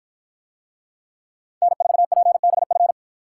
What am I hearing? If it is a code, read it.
Morse code, heard as a keyed tone, about 53 wpm, 700 Hz. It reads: N4PBF